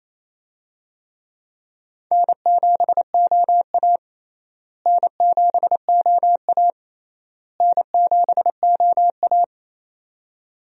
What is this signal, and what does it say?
Morse code, heard as a keyed tone, about 28 wpm, 700 Hz. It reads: N7OA N7OA N7OA
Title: USA Call Signs - 4 Characters - Repeated 3x - 28wpm